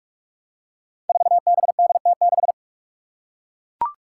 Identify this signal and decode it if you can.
Morse code, heard as a keyed tone, about 42 wpm, 700 Hz. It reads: 4BDT6
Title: Five Challenging Letter-Number Combinations 45wpm